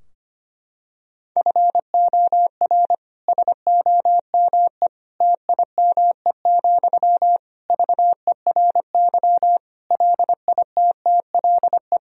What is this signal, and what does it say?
Morse code, heard as a keyed tone, about 25 wpm, 700 Hz. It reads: FOR SOME TIME, VERY LITTLE
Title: Sentences from Top 1000 Words - Review Part 4 - 25wpm